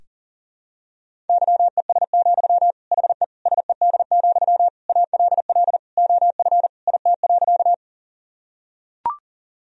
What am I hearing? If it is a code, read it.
Morse code, heard as a keyed tone, about 40 wpm, 700 Hz. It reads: YES, HE SED, ALL OF IT.
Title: Sentences from Top 500 Words and Abbreviations 40wpm